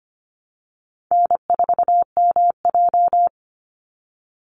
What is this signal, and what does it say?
Morse code, heard as a keyed tone, about 25 wpm, 700 Hz. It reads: N4MJ
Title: NA SOTA Chaser Callsigns 25wpm